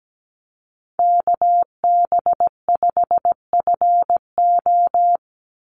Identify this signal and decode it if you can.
Morse code, heard as a keyed tone, about 17 wpm, 700 Hz. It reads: KB5FO
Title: USA Call Signs 17wpm